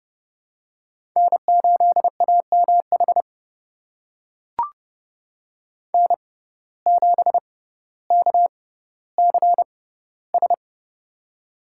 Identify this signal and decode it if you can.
Morse code, heard as a keyed tone, about 32 wpm, spaced as for 16 wpm, 700 Hz. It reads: N8AMH N7KCS
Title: USA Call Signs - 5 Characters - 6x Character Spacing - 30wpm